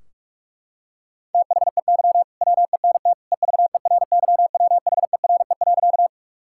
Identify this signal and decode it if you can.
Morse code, heard as a keyed tone, about 48 wpm, 700 Hz. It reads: THEY WENT EVERYWHERE.
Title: Sentences from Top 1600 Words 45wpm